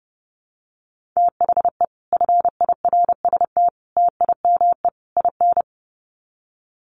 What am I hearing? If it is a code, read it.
Morse code, heard as a keyed tone, about 28 wpm, 700 Hz. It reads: THE FIRST TIME IN
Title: Sets of 4 Words 30wpm